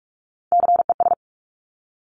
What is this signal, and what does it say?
Morse code, heard as a keyed tone, about 46 wpm, 700 Hz. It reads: CES
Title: Top Three Letter Combinations in Words 45wpm